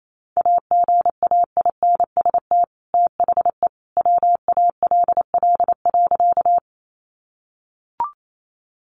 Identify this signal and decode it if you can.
Morse code, heard as a keyed tone, about 28 wpm, 700 Hz. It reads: AGAINST THE WALL.
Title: Sentences from Top 1100 Words 28wpm